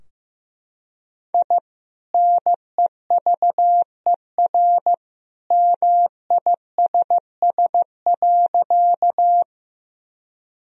Morse code, heard as a keyed tone, about 15 wpm, 700 Hz. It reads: I NEVER MISS.
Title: Sentences from Top 1000 Words - Review Part 3 - 15wpm